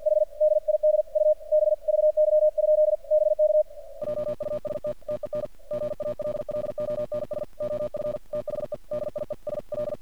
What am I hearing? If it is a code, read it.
Morse code, heard as a keyed tone, about 46 wpm, 610 Hz. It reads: H DID U B VY? BK JUST TEN GALLONS OF THE DIESL
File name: archived / CRSnet